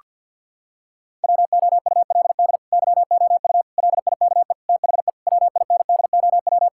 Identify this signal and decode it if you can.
Morse code, heard as a keyed tone, about 50 wpm, 700 Hz. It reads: WOULD YOU LIKE THE WINDOW
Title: Sentences from Top 1600 Words 50wpm